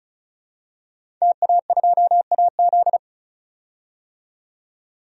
Morse code, heard as a keyed tone, about 34 wpm, 700 Hz. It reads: TA2AZ
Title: DXCC-100 Call Signs - 5 Characters - 35wpm